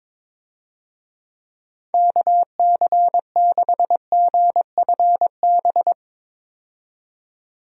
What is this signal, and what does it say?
Morse code, heard as a keyed tone, about 21 wpm, 700 Hz. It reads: KC6GFB
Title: USA Call Signs - 6 Characters - 22wpm